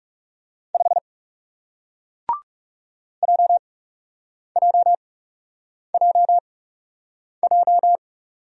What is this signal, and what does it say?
Morse code, heard as a keyed tone, about 40 wpm, spaced as for 8 wpm, 700 Hz. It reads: 5 JJJJ